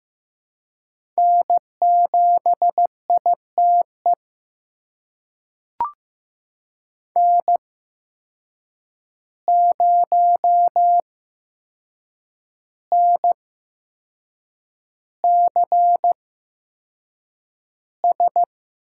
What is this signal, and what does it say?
Morse code, heard as a keyed tone, about 15 wpm, spaced as for 6 wpm, 700 Hz. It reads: N7ITE N0NCS
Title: USA Call Signs - 5 Characters - 8x Character Spacing - 15wpm